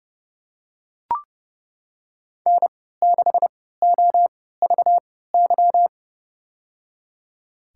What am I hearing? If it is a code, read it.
Morse code, heard as a keyed tone, about 30 wpm, spaced as for 17 wpm, 700 Hz. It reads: N6OVY
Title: USA Call Signs - 5 Characters - 3x Character Spacing - 30wpm